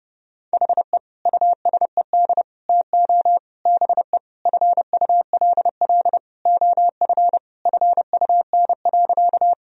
Morse code, heard as a keyed tone, about 30 wpm, 700 Hz. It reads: HE USED TO BE FULL OF FUN.
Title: Sentences from Top 1000 Words - Review Part 4 - 30wpm